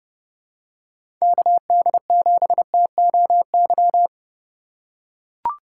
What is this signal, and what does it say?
Morse code, heard as a keyed tone, about 30 wpm, 700 Hz. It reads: KD7TOY
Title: USA Call Signs - 4x Character Spacing - 30wpm